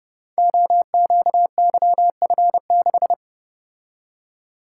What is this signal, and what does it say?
Morse code, heard as a keyed tone, about 30 wpm, 700 Hz. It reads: OQYF6
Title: Code Groups - 5 Letter:Numbers - 30wpm